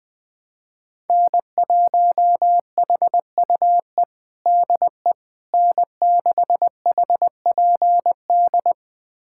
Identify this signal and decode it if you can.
Morse code, heard as a keyed tone, about 20 wpm, 700 Hz. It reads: N1HUE DE N6HPD
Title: USA Callsign DE Callsign 20wpm